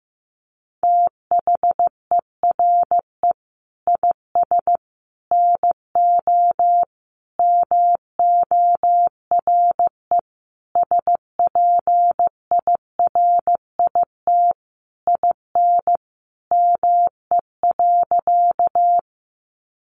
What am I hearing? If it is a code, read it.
Morse code, heard as a keyed tone, about 15 wpm, 700 Hz. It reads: THERE IS NO MORE SPIRIT IN ME.
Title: Sentences from Top 1400 Words 15wpm